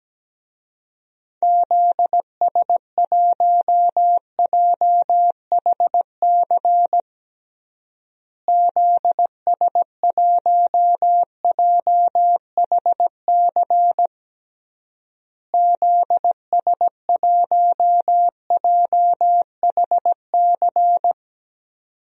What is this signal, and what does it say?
Morse code, heard as a keyed tone, about 17 wpm, 700 Hz. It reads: ZS1JHC ZS1JHC ZS1JHC
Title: DXCC-100 Call Signs - 6 Characters - Repeated 3x - 17wpm